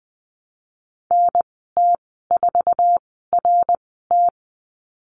Morse code, heard as a keyed tone, about 20 wpm, spaced as for 15 wpm, 700 Hz. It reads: NT4RT
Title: 055 Lesson 10g - Letter r - Call Signs 20wpm